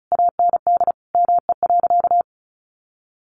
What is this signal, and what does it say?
Morse code, heard as a keyed tone, about 34 wpm, 700 Hz. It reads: AND ME.
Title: Sentences from Top 1200 Words 35wpm